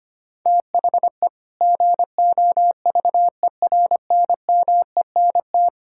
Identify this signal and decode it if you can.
Morse code, heard as a keyed tone, about 25 wpm, 700 Hz. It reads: THE GOVERNMENT
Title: Sentences from Top 1000 Words - Review Part 5 - 25wpm